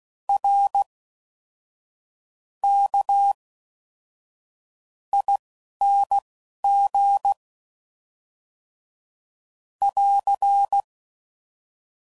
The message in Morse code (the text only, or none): RKING <AR>